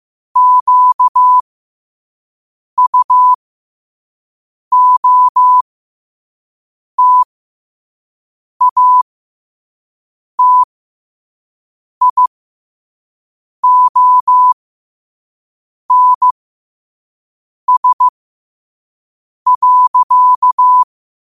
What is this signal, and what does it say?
Morse code, heard as a keyed tone, about 15 wpm, spaced as for 5 wpm, 1000 Hz. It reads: QUOTATIONS.
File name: qotdfiles / 5wpm